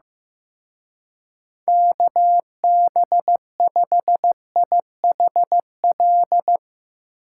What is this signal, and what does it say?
Morse code, heard as a keyed tone, about 16 wpm, 700 Hz. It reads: KB5IHL
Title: Speed Racing - USA 2x3 Call Signs 15wpm